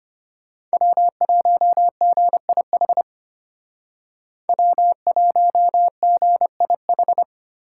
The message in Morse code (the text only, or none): W1GIH W1GIH